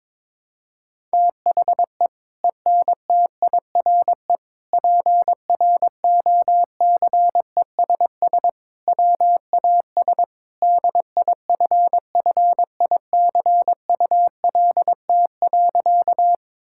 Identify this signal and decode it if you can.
Morse code, heard as a keyed tone, about 22 wpm, 700 Hz. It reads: THE ENTIRE PROCESS WAS DIFFICULT.
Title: Sentences from Top 1000 Words - Review Part 5 - 22wpm